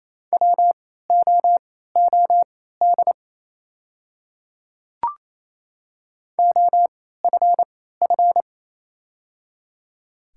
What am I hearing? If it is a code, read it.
Morse code, heard as a keyed tone, about 27 wpm, spaced as for 16 wpm, 690 Hz. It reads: WOOD OFF
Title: 3x Character Spacing - Top 300 Words - 28wpm